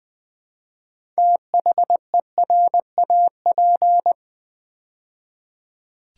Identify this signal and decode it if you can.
Morse code, heard as a keyed tone, about 20 wpm, 700 Hz. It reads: THERAP
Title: Build A Word - 3 Letters At A Time - 9-Letter Words - 20wpm